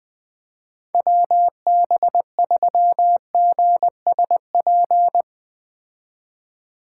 Morse code, heard as a keyed tone, about 20 wpm, 700 Hz. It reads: WB3GSP